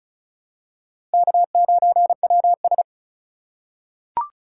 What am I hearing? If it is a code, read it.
Morse code, heard as a keyed tone, about 36 wpm, 700 Hz. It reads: K9WS
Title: USA Call Signs - 4 Characters - 3x Character Spacing - 35wpm